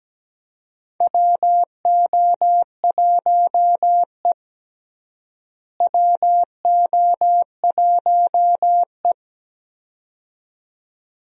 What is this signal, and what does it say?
Morse code, heard as a keyed tone, about 17 wpm, 700 Hz. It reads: WO1E WO1E
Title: USA Call Signs - 4 Characters - Repeated 2x - 17wpm